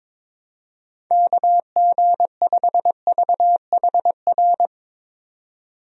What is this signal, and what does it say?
Morse code, heard as a keyed tone, about 21 wpm, 700 Hz. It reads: KG5VHR
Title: USA Call Signs - 6 Characters - 22wpm